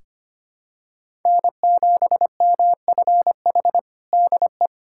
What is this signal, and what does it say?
Morse code, heard as a keyed tone, about 25 wpm, 700 Hz. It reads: N7MFH DE
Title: USA Callsign DE Callsign 25wpm